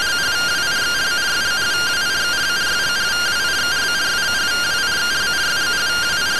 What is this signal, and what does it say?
Two-tone teleprinter signal, about 170 Hz shift.